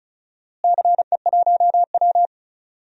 Morse code, heard as a keyed tone, about 36 wpm, 700 Hz. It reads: CE1W
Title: DXCC-100 Call Signs - 4 Characters - Repeated 2x - 35wpm